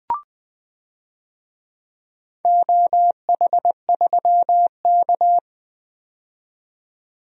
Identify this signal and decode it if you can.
Morse code, heard as a keyed tone, about 20 wpm, 700 Hz. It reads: OH3K
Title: DXCC-100 Call Signs - 4 Characters - 20wpm